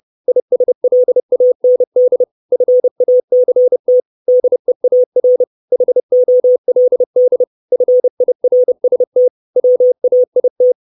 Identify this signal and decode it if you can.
Morse code, heard as a keyed tone, about 30 wpm, 500 Hz. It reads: ISLAND FACT DEAR HOLD FIRST WAIT
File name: 632 Words 30WPM